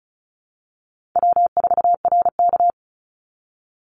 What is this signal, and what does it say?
Morse code, heard as a keyed tone, about 34 wpm, 700 Hz. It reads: W4RK
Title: Easy Call Signs Repeated 3x - 35wpm